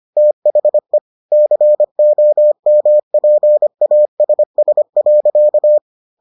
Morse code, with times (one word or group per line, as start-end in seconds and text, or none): 0.2-1.0 THE
1.3-5.8 COMPASS.